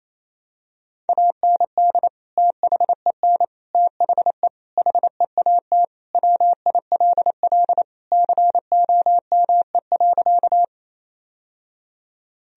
Morse code, heard as a keyed tone, about 28 wpm, 700 Hz. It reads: AND THEN THE HEAT WILL COME.
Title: Sentences from Top 1000 Words - Review Part 3 - 28wpm